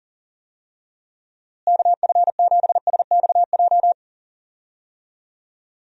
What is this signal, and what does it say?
Morse code, heard as a keyed tone, about 40 wpm, 700 Hz. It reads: KF7SXJ
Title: USA Call Signs - 6 Characters - Repeated 2x - 40wpm